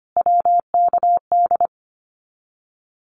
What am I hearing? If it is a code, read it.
Morse code, heard as a keyed tone, about 25 wpm, 700 Hz. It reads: WKD
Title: Abbreviations Repeated 3x - 25wpm